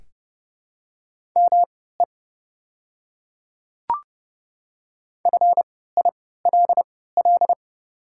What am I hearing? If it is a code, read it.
Morse code, heard as a keyed tone, about 30 wpm, spaced as for 17 wpm, 700 Hz. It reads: ME FILL
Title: 3x Character Spacing - Top 500 Words - 30wpm